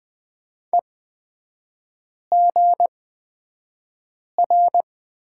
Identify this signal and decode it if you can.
Morse code, heard as a keyed tone, about 20 wpm, spaced as for 5 wpm, 690 Hz. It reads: EGR
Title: 236 Lesson 38e - Letter q - Three Characters 20wpm